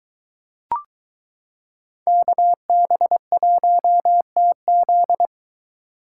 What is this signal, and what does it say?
Morse code, heard as a keyed tone, about 23 wpm, 700 Hz. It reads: KB1TZ